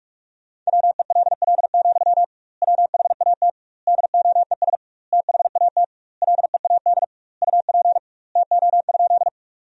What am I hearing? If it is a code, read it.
Morse code, heard as a keyed tone, about 41 wpm, 700 Hz. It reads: WELL, WHAT DOES THAT LEAD UP TO?
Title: 2x Word-Spacing - Sentences from Top 500 Words - 45wpm